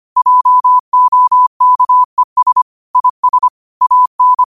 Morse code, heard as a keyed tone, about 25 wpm, 1000 Hz. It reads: JOKES IS AN
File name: qotdfiles / 25wpm